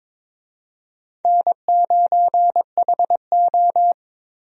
Morse code, heard as a keyed tone, about 22 wpm, 700 Hz. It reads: N9HO
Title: USA Call Signs - 4 Characters - Repeated 2x - 22wpm